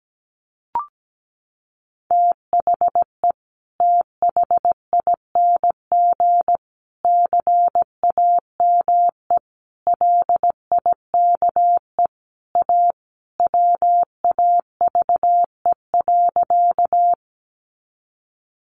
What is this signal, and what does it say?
Morse code, heard as a keyed tone, about 17 wpm, 700 Hz. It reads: THE THING CAME LIKE A WAVE.
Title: Sentences from Top 1400 Words 17wpm